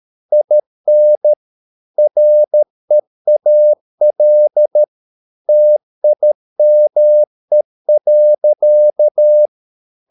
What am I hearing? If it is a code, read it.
Morse code, heard as a keyed tone, about 13 wpm, 600 Hz. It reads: IN REAL TIME.